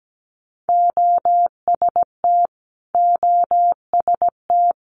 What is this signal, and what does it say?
Morse code, heard as a keyed tone, about 17 wpm, 700 Hz. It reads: OST OST
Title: Multi-letter Phonemes Repeated 3x 17wpm